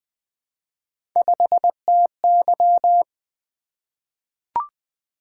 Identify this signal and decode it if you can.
Morse code, heard as a keyed tone, about 20 wpm, 700 Hz. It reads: 5TY
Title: Three Letter-Number Combinations - 20wpm